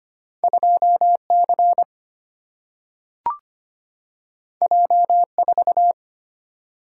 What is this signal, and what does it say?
Morse code, heard as a keyed tone, about 25 wpm, 700 Hz. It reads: 2C J4
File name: Two Letter-Number Combinations - 25wpm